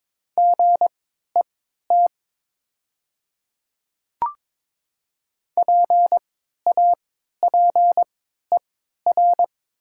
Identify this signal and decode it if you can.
Morse code, heard as a keyed tone, about 21 wpm, spaced as for 12 wpm, 700 Hz. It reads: GET PAPER
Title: 3x Character Spacing - Top 300 Words - 22wpm